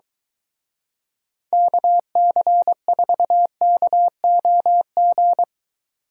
Morse code, heard as a keyed tone, about 23 wpm, 700 Hz. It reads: KC4KOG